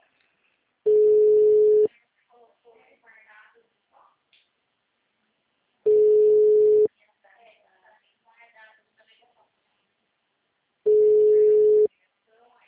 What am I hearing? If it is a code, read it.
Ringback tone (Europe, 425 Hz), with speech.